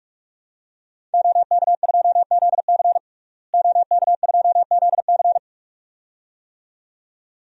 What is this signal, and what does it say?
Morse code, heard as a keyed tone, about 46 wpm, 700 Hz. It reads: OK2ZC OK2ZC
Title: DXCC-100 Call Signs - 5 Characters - Repeated 2x - 45wpm